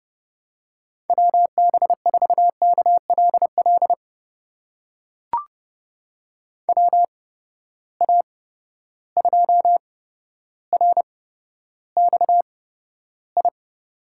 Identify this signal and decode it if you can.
Morse code, heard as a keyed tone, about 32 wpm, spaced as for 13 wpm, 700 Hz. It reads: WB4KLL WA2RXI
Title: USA Call Signs - 6 Characters - 8x Character Spacing - 30wpm